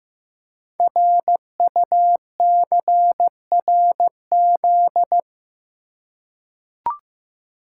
Morse code, heard as a keyed tone, about 15 wpm, 700 Hz. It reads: RUCRZ
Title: Code Groups - 5 Letters - 15wpm